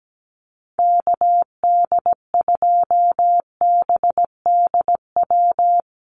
Morse code, heard as a keyed tone, about 17 wpm, 700 Hz. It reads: KD2BDW
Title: USA Call Signs Repeated 3x 17wpm